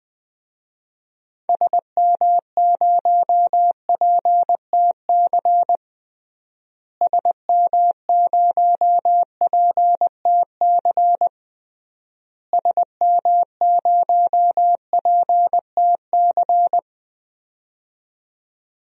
Morse code, heard as a keyed tone, about 20 wpm, 700 Hz. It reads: SM0PTC SM0PTC SM0PTC
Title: DXCC-100 Call Signs - 6 Characters - Repeated 3x - 20wpm